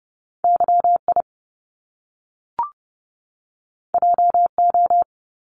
Morse code, heard as a keyed tone, about 30 wpm, 700 Hz. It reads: YI JO
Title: Top Two Letter Combinations in Words 30wpm